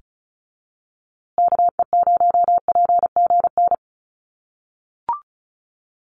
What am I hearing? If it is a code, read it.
Morse code, heard as a keyed tone, about 34 wpm, 700 Hz. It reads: KE0PGN
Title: USA Call Signs Repeated 3x 35wpm